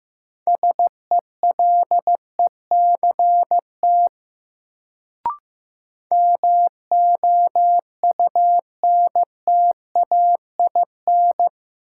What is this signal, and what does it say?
Morse code, heard as a keyed tone, about 15 wpm, 700 Hz. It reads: SELECT MOUNTAIN
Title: Sentences from Top 1300 Words - New Word Drill Alt - 15wpm